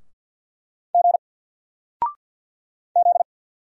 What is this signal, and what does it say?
Morse code, heard as a keyed tone, about 50 wpm, 700 Hz. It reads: G Z
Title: Single Letter-Number 50wpm